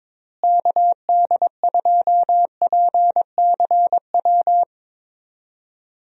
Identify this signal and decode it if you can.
Morse code, heard as a keyed tone, about 22 wpm, 700 Hz. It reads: KD2PCW
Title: USA Call Signs - 6 Characters - Repeated 2x - 22wpm